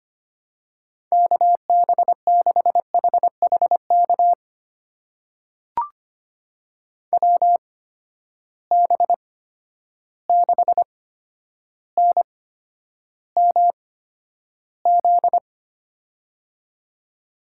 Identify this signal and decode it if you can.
Morse code, heard as a keyed tone, about 25 wpm, spaced as for 11 wpm, 700 Hz. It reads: KB6HHK WB6NMZ
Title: USA Call Signs - 6 Characters - 8x Character Spacing - 25wpm